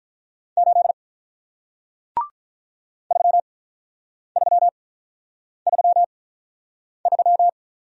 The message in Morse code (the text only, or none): 8 3333